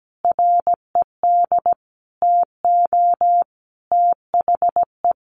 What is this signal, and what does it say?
Morse code, heard as a keyed tone, about 17 wpm, 700 Hz. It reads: RED TO THE
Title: A Sound Of Thunder 17wpm